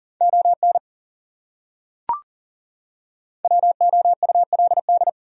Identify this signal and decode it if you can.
Morse code, heard as a keyed tone, about 37 wpm, 700 Hz. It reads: ON WOULD
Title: Top 100 Words 40wpm